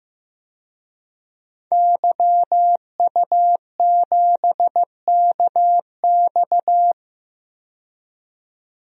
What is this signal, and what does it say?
Morse code, heard as a keyed tone, about 15 wpm, 700 Hz. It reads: YU7KX
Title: DXCC-100 Call Signs - 5 Characters - 15wpm